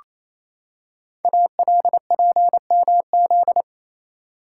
Morse code, heard as a keyed tone, about 28 wpm, 700 Hz. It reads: ALPMZ
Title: Code Groups - 5 Letter:Numbers - 28wpm